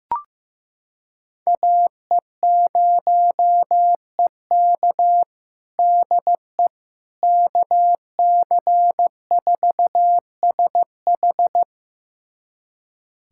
Morse code, heard as a keyed tone, about 15 wpm, 700 Hz. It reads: AE0EK DE KC4SH
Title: USA Callsign DE Callsign - 5 Characters - 15wpm